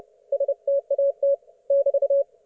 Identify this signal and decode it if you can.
Morse code, heard as a keyed tone, about 30 wpm, 560 Hz. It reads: STAT <BT>
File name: Archiv / CUC179 / ok1fhi cul179